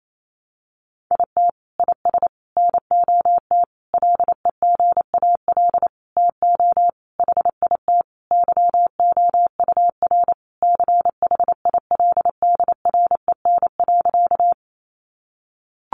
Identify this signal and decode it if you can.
Morse code, heard as a keyed tone, about 28 wpm, 700 Hz. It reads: IT IS NOT LEGAL TO HIT YOUR CHILDREN.
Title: Sentences from Top 1200 Words 28wpm